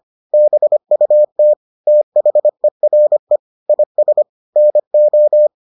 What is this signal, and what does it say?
Morse code, heard as a keyed tone, about 25 wpm, 600 Hz. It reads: BUT THERE IS NO